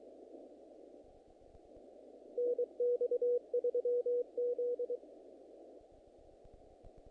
Morse code, heard as a keyed tone, about 23 wpm, 500 Hz, with radio noise. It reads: NX3Z
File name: contests / naqp-cw-k5tr-op-k5pi- -2000